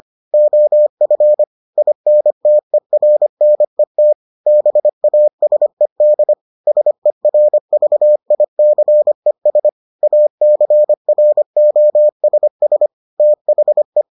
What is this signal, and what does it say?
Morse code, heard as a keyed tone, about 25 wpm, 600 Hz. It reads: OF INTERNET BASED SERVICES ACROSS THE